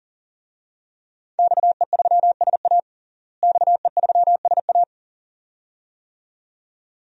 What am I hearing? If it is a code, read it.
Morse code, heard as a keyed tone, about 40 wpm, 700 Hz. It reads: XE3SA XE3SA
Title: DXCC-100 Call Signs - 5 Characters - Repeated 2x - 40wpm